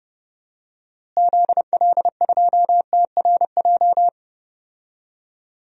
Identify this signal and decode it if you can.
Morse code, heard as a keyed tone, about 30 wpm, 700 Hz. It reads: ZL2TRJ
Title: DXCC-100 Call Signs - 6 Characters - Repeated 2x - 30wpm